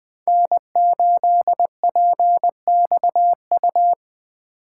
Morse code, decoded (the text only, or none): N8PXU